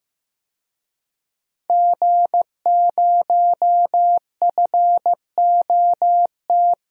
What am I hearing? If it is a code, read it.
Morse code, heard as a keyed tone, about 15 wpm, 700 Hz. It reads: G0FOT
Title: DXCC-100 Call Signs - 5 Characters - Repeated 3x - 15wpm